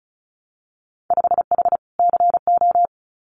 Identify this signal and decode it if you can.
Morse code, heard as a keyed tone, about 36 wpm, 700 Hz. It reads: 5H CO
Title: Winter Field Day Exchanges Encore 35wpm